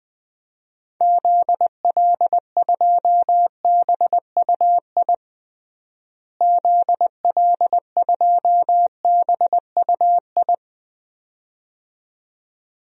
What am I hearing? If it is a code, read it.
Morse code, heard as a keyed tone, about 21 wpm, 700 Hz. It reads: ZL2BUI ZL2BUI
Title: DXCC-100 Call Signs - 6 Characters - Repeated 2x - 20wpm